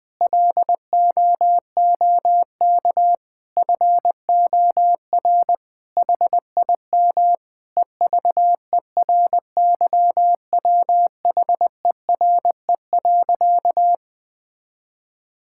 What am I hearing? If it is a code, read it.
Morse code, heard as a keyed tone, about 21 wpm, 700 Hz. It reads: LOOK FOR HIM EVERYWHERE.
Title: Sentences from Top 1600 Words 20wpm